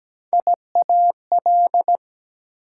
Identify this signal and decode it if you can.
Morse code, heard as a keyed tone, about 17 wpm, 700 Hz. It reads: IAL
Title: Prefixes and Suffixes 17wpm